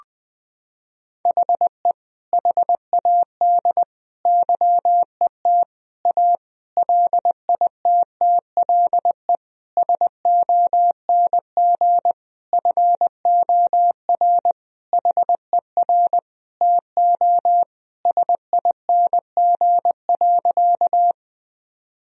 Morse code, heard as a keyed tone, about 20 wpm, 700 Hz. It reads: HE HAD YET A LITTLE SONG FOR HER TO SING.
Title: Sentences from Top 1000 Words - Review Part 2 - 20wpm